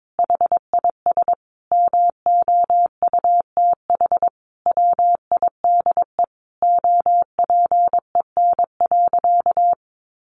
Morse code, heard as a keyed tone, about 22 wpm, 700 Hz. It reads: HIS MOUTH WIDE OPEN.
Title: Sentences from Top 1500 Words 22wpm